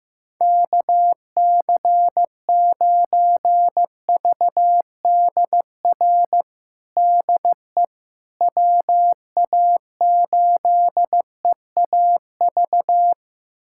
Morse code, heard as a keyed tone, about 15 wpm, 700 Hz. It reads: KC9VDR DE WA8EAV
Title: USA Callsign DE Callsign 15wpm